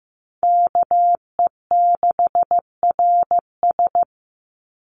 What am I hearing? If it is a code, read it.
Morse code, heard as a keyed tone, about 15 wpm, 700 Hz. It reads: KE6RS
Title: NA SOTA Chaser Callsigns 15wpm